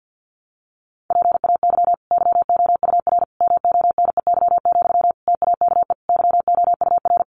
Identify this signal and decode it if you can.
Morse code, heard as a keyed tone, about 49 wpm, 700 Hz. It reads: PAY YOUR MONEY, TAKE YOUR
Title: Sentences from Top 900 Words 50wpm